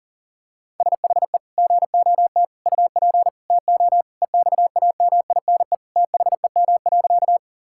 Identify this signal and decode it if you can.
Morse code, heard as a keyed tone, about 40 wpm, 700 Hz. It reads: SHE GOT UP TO EXAMINE THEM.
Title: Sentences from Top 1600 Words 40wpm